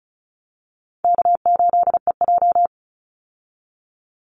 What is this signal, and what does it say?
Morse code, heard as a keyed tone, about 34 wpm, 700 Hz. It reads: K8EJ
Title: Easy Call Signs Repeated 3x - 35wpm